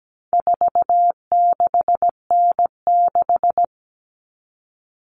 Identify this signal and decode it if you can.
Morse code, heard as a keyed tone, about 17 wpm, 700 Hz. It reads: 46N6
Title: Common Cut Numbers - 4-Digit 17wpm